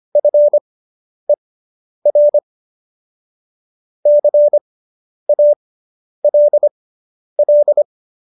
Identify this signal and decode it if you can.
Morse code, heard as a keyed tone, about 25 wpm, spaced as for 10 wpm, 590 Hz. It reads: FER CALL
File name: QSO220 10